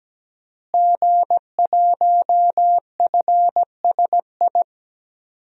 Morse code, heard as a keyed tone, about 17 wpm, 700 Hz. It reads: G1FSI
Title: DXCC-100 Call Signs - 5 Characters - Repeated 2x - 17wpm